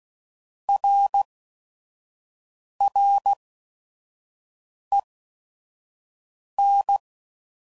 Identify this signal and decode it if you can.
Morse code, heard as a keyed tone, about 16 wpm, 800 Hz. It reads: R R E N